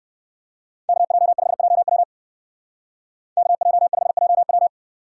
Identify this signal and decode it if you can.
Morse code, heard as a keyed tone, about 58 wpm, 700 Hz. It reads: KJ5JW KJ5JW